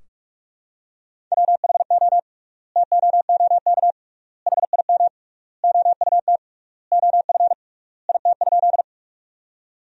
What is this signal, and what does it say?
Morse code, heard as a keyed tone, about 46 wpm, 700 Hz. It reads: WHO TOOK HIM OUT OF IT?
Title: 3x Word-Spacing - Sentences from Top 300 Words - 45wpm